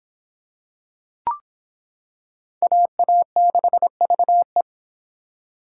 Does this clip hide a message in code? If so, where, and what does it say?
Yes: Morse code from 2.6 to 4.6 s, heard as a keyed tone, about 25 wpm, 700 Hz. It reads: AA6VE